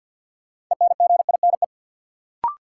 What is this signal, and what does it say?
Morse code, heard as a keyed tone, about 52 wpm, 700 Hz. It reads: ENGINE
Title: Sentences from Top 1500 Words - New Word Drill Alt - 50wpm